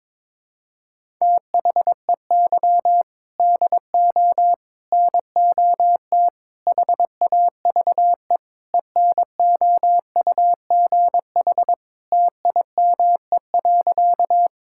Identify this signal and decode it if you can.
Morse code, heard as a keyed tone, about 22 wpm, 700 Hz. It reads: THEY DO NOT HAVE ENOUGH TIME.
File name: practice / 22wpm / Sentences with 4-Word Phrases - 22wpm